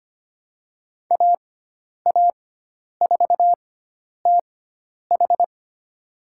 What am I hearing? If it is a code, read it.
Morse code, heard as a keyed tone, about 25 wpm, spaced as for 10 wpm, 700 Hz. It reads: AA4TH
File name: Lesson 16g - Letter c - Call Signs 25wpm